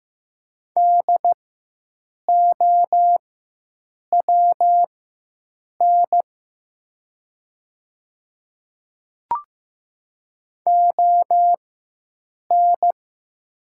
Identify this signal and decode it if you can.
Morse code, heard as a keyed tone, about 15 wpm, spaced as for 7 wpm, 700 Hz. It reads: DOWN ON
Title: 4x Character Spacing - Top 100 Words - 15wpm